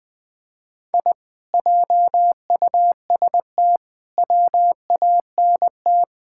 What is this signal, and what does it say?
Morse code, heard as a keyed tone, about 20 wpm, 700 Hz. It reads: I JUST WANT
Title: Sentences with 4-Word Phrases - 20wpm